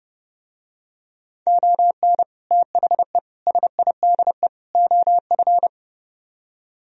Morse code, heard as a keyed tone, about 32 wpm, 700 Hz. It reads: ON THE SIDE OF